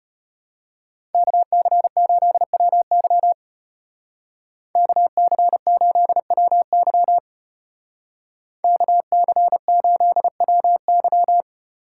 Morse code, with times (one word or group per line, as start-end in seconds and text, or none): 1.1-3.3 KC8WY
4.8-7.2 KC8WY
8.6-11.4 KC8WY